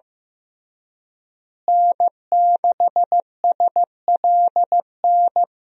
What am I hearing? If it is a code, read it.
Morse code, heard as a keyed tone, about 15 wpm, 700 Hz. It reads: N6SLN
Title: USA Call Signs - 5 Characters - Repeated 2x - 15wpm